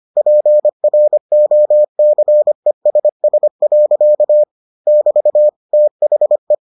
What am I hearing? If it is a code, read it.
Morse code, heard as a keyed tone, about 25 wpm, 600 Hz. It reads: PROCESS. <BT> THE